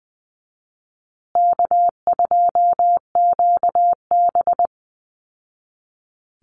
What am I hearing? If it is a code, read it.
Morse code, heard as a keyed tone, about 20 wpm, 700 Hz. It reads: K2QB
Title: Easy Call Signs Repeated 2x - 20wpm